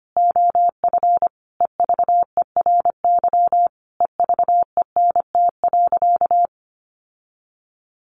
Morse code, heard as a keyed tone, about 25 wpm, 700 Hz. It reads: OF EVERY EVENT.
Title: Sentences from Top 1400 Words 25wpm